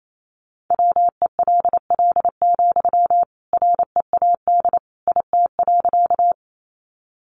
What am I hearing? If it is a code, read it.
Morse code, heard as a keyed tone, about 28 wpm, 700 Hz. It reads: WELL, READ IT.
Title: Sentences from Top 200 Words 28wpm